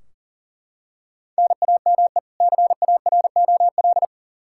Morse code, heard as a keyed tone, about 40 wpm, 700 Hz. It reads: NAME CAROL
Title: Practice QSO - Exchange 1 - Beginning - 40wpm